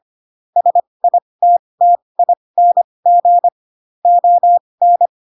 Morse code, heard as a keyed tone, about 25 wpm, spaced as for 20 wpm, 700 Hz. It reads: SITTING ON
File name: Pinocchio / Pinocchio ch1 15 25 ik1hge